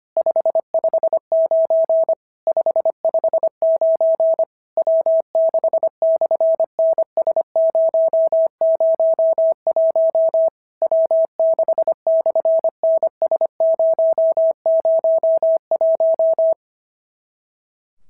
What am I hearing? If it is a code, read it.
Morse code, heard as a keyed tone, about 26 wpm, 650 Hz. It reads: 559 559 W6/NS001 W6/NS001
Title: SOTA QSO Practice 25wpm